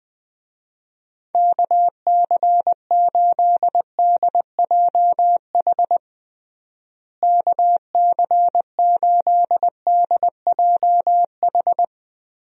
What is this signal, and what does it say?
Morse code, heard as a keyed tone, about 20 wpm, 700 Hz. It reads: KC8DJH KC8DJH
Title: USA Call Signs - 6 Characters - Repeated 3x - 20wpm